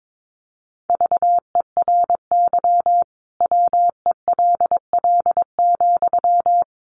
Morse code, heard as a keyed tone, about 22 wpm, 700 Hz. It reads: VERY WELL,
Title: Sentences from Top 200 Words 22wpm